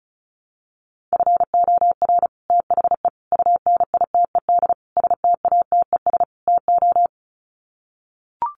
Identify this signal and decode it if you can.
Morse code, heard as a keyed tone, about 34 wpm, 700 Hz. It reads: FOR THE UNITED STATES TO
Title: Sets of 5 Words Encore 35wpm